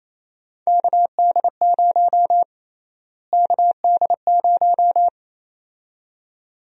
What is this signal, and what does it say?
Morse code, heard as a keyed tone, about 27 wpm, 700 Hz. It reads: KD0 KD0
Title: USA Call Sign Prefixes - Repeated 2x - 28wpm